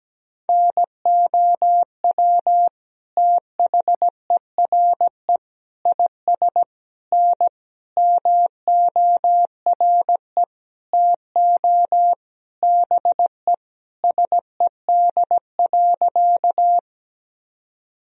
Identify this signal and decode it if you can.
Morse code, heard as a keyed tone, about 17 wpm, 700 Hz. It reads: NOW THERE IS N MORE TO BE SED.
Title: Sentences from Top 100 Words and Abbreviations 17wpm